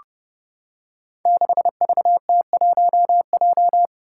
Morse code, heard as a keyed tone, about 28 wpm, 700 Hz. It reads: 6VT1J
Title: Five Challenging Letter-Number Combinations 30wpm